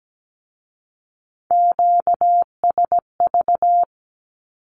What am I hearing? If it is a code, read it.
Morse code, heard as a keyed tone, about 17 wpm, 700 Hz. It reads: QSV
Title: Q-Codes with Spoken Meaning - 17wpm